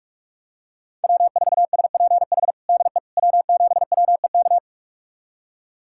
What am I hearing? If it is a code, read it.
Morse code, heard as a keyed tone, about 46 wpm, 700 Hz. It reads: W4SPH DE W7WEK
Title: USA Callsign DE Callsign - 5 Characters - 45wpm